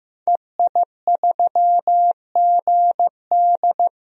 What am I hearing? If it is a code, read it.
Morse code, heard as a keyed tone, about 16 wpm, 700 Hz. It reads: EI3GD
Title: DXCC-100 Call Signs - 5 Characters - Repeated 2x - 15wpm